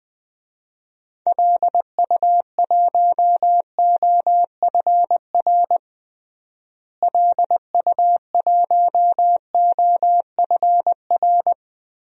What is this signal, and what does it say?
Morse code, heard as a keyed tone, about 20 wpm, 700 Hz. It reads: LU1OFR LU1OFR
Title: DXCC-100 Call Signs - 6 Characters - Repeated 3x - 20wpm